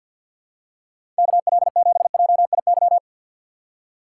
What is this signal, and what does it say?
Morse code, heard as a keyed tone, about 53 wpm, 700 Hz. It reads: KL7JIY
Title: USA Call Signs - 6 Characters - 4x Character Spacing - 50wpm